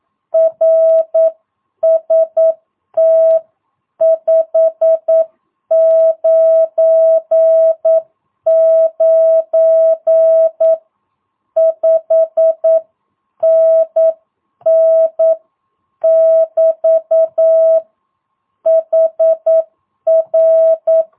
Morse code, heard as a keyed tone, about 9 wpm, 650 Hz. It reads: RST599 5NN <BT> HR
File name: jh8cbh / 2022cw